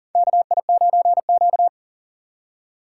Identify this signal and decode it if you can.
Morse code, heard as a keyed tone, about 42 wpm, 700 Hz. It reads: KI9Q
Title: USA Call Signs - 4 Characters - 3x Character Spacing - 40wpm